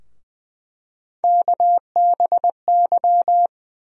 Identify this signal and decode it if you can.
Morse code, heard as a keyed tone, about 20 wpm, 700 Hz. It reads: KBY
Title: USA Call Sign Postfixes - 3 Letters 20wpm